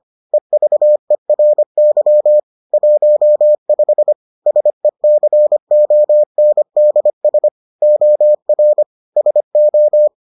Morse code, heard as a keyed tone, about 25 wpm, 600 Hz. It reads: EVERY 15 SECONDS OR SO